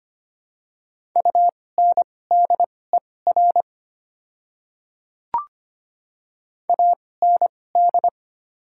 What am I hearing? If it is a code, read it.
Morse code, heard as a keyed tone, about 25 wpm, spaced as for 18 wpm, 700 Hz. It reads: UNDER AND
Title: 2x Character Spacing - Top 300 Words - 25wpm